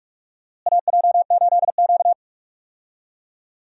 Morse code, heard as a keyed tone, about 46 wpm, 700 Hz. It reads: AJ8Q
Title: USA Call Signs - 4 Characters - Repeated 3x - 45wpm